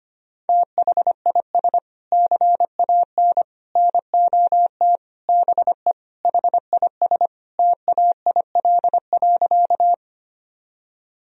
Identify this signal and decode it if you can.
Morse code, heard as a keyed tone, about 25 wpm, 700 Hz. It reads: THIS CAN NOT BE HIS TAIL.
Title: Sentences from Top 1000 Words - Review Part 3 - 25wpm